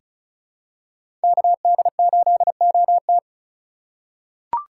Morse code, heard as a keyed tone, about 36 wpm, 700 Hz. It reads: KD8OT
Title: USA Call Signs - 5 Characters - 3x Character Spacing - 35wpm